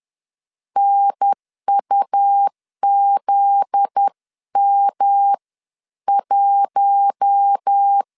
Morse code, heard as a keyed tone, about 10 wpm, spaced as for 9 wpm, 800 Hz. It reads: NUZM1
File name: LEZIONI / 8 YJ45 / 8 YJ4 10wpm 2633